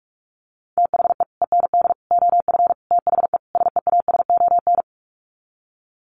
Morse code, heard as a keyed tone, about 46 wpm, 700 Hz. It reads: THE END OF THE SEASON
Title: Sets of 5 Words Encore 45wpm